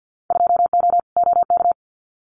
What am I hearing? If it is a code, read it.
Morse code, heard as a keyed tone, about 50 wpm, 700 Hz. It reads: 2O OK
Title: Winter Field Day Exchanges Encore 50wpm